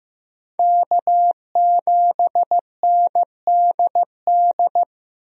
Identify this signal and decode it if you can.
Morse code, heard as a keyed tone, about 15 wpm, 700 Hz. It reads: K7NDD
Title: USA Call Signs - 5 Characters - Repeated 3x - 15wpm